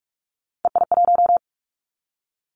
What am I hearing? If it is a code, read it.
Morse code, heard as a keyed tone, about 46 wpm, 700 Hz. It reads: EI1
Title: Callsign Prefixes for Top 50 DXCC - 45wpm